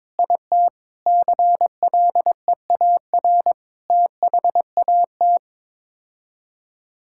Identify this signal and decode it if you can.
Morse code, heard as a keyed tone, about 22 wpm, 700 Hz. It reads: IT CLEAR THAT
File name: practice / 22wpm / Sets of 7 Words - 22wpm